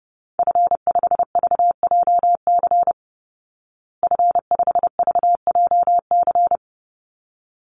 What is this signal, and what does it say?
Morse code, heard as a keyed tone, about 30 wpm, 700 Hz. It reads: F5VJC F5VJC
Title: Call Signs Repeated 3x - 30wpm